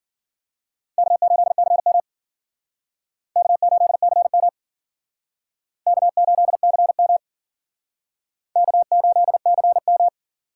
Morse code, heard as a keyed tone, about 51 wpm, 700 Hz. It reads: K8CM K8CM K8CM K8CM